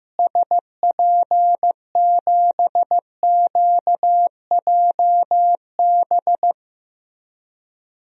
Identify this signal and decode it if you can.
Morse code, heard as a keyed tone, about 15 wpm, 700 Hz. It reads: SP7QJB
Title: DXCC-100 Call Signs - 6 Characters - Repeated 3x - 15wpm